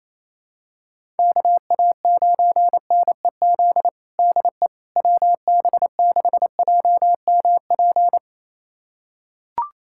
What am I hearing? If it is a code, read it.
Morse code, heard as a keyed tone, about 27 wpm, 700 Hz. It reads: KA9NEZ DE WB6JMP
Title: USA Callsign DE Callsign 28wpm